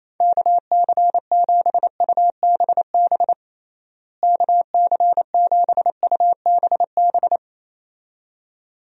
Morse code, heard as a keyed tone, about 28 wpm, 700 Hz. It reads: KC7UBB KC7UBB
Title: USA Call Signs - 6 Characters - Repeated 2x - 28wpm